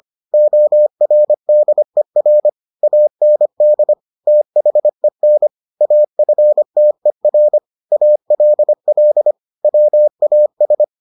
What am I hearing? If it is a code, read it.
Morse code, heard as a keyed tone, about 25 wpm, 600 Hz. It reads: ORDER AND THEN AFTER ALL WAS